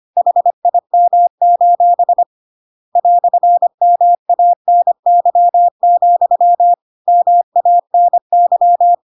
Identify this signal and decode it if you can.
Morse code, heard as a keyed tone, about 25 wpm, 700 Hz. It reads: HIM: "MANY, MANY
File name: Pinocchio / Pinocchio ch5 25 25 ik1hge